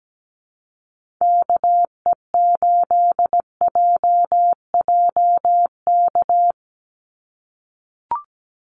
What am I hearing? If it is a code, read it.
Morse code, heard as a keyed tone, about 17 wpm, 700 Hz. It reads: KE8JJK
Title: USA Call Signs 17wpm